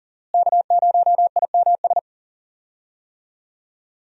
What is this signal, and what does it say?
Morse code, heard as a keyed tone, about 38 wpm, 700 Hz. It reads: K0IMS
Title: USA Call Signs - 5 Characters - Repeated 2x - 40wpm